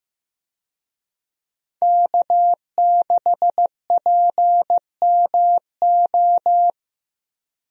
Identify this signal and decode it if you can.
Morse code, heard as a keyed tone, about 15 wpm, 700 Hz. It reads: K6PMO
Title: USA Call Signs - 5 Characters - Repeated 2x - 15wpm